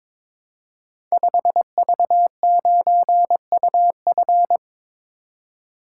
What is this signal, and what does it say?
Morse code, heard as a keyed tone, about 22 wpm, 700 Hz. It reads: 5V9UF
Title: Code Groups - 5 Letter:Numbers - 22wpm